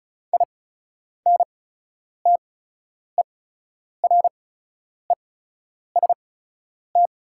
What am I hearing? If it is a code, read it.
Morse code, heard as a keyed tone, about 36 wpm, spaced as for 10 wpm, 690 Hz. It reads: INTEREST
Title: 8x Character Spacing - Top 500 Words - 35wpm